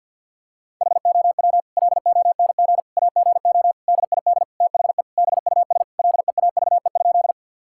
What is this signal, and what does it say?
Morse code, heard as a keyed tone, about 50 wpm, 700 Hz. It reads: HOW LONG AGO DID THE BUS LEAVE?
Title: Sentences from Top 1000 Words - Review Part 6 - 50wpm